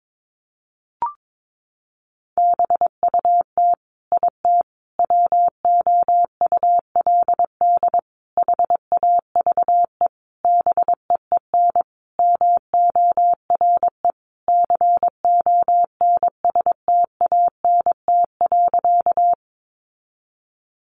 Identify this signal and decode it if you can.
Morse code, heard as a keyed tone, about 22 wpm, 700 Hz. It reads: BUT IT WOULD HAVE BEEN MORE CONSTANT.
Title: Sentences from Top 1500 Words 22wpm